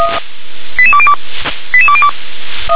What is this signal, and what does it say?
Five-tone selective call (ZVEI), heard as quick runs of five tones, tones of about 70 ms. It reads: 88202 88202